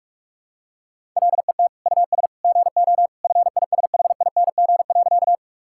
Morse code, heard as a keyed tone, about 42 wpm, 700 Hz. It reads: LET US GO FISHING.
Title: Sentences from Top 1000 Words - Review Part 5 - 45wpm